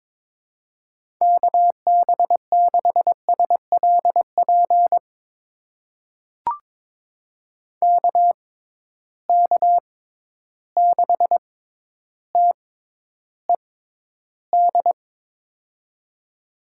Morse code, heard as a keyed tone, about 22 wpm, spaced as for 11 wpm, 700 Hz. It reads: KB6SLP KK6TED
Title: USA Call Signs - 6x Character Spacing - 22wpm